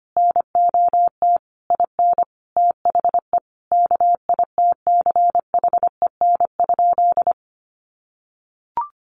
Morse code, heard as a keyed tone, about 25 wpm, 700 Hz. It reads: NOT IN THE KITCHEN?
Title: Sentences from Top 1400 Words 25wpm